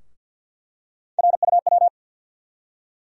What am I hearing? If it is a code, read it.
Morse code, heard as a keyed tone, about 47 wpm, 700 Hz. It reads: RUW
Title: USA Call Sign Postfixes - 3 Letters 50wpm